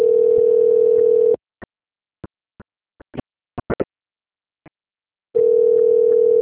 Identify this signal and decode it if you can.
Ringback tone (North America), 440 + 480 Hz.
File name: out- - - - - .7277